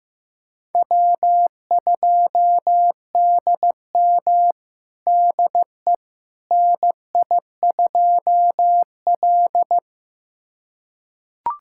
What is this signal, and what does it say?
Morse code, heard as a keyed tone, about 15 wpm, 700 Hz. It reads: W2DM DE NI2L
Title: USA Callsign DE Callsign - 4 Characters - 15wpm